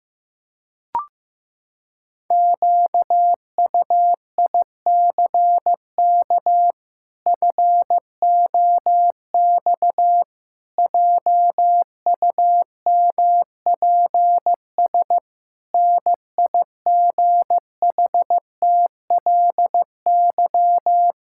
Morse code, heard as a keyed tone, about 15 wpm, 700 Hz. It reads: QUICK FOX JUMPS NIGHTLY
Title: Pangram Challenge v2 - 15wpm